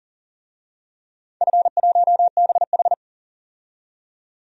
Morse code, heard as a keyed tone, about 40 wpm, 700 Hz. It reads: F1BH
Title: DXCC-100 Call Signs - 4 Characters - 40wpm